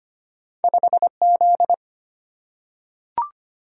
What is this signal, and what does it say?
Morse code, heard as a keyed tone, about 25 wpm, 700 Hz. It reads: 5Z
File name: Two Letter-Number Combinations - 25wpm